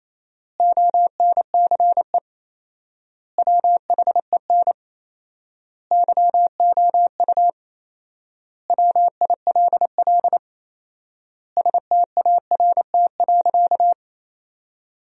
Morse code, heard as a keyed tone, about 28 wpm, 700 Hz. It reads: ONCE WHEN YOU WILL START.
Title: 4x Word-Spacing - Sentences from Top 300 Words - 28wpm